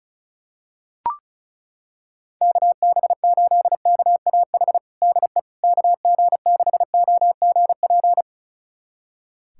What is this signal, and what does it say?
Morse code, heard as a keyed tone, about 36 wpm, 700 Hz. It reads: KB8KAH DE KG6OGP
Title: USA Callsign DE Callsign - 6 Characters - 35wpm